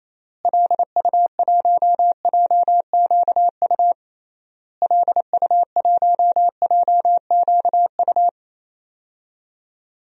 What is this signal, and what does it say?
Morse code, heard as a keyed tone, about 28 wpm, 700 Hz. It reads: LU1JQU LU1JQU
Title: DXCC-100 Call Signs - 6 Characters - Repeated 3x - 28wpm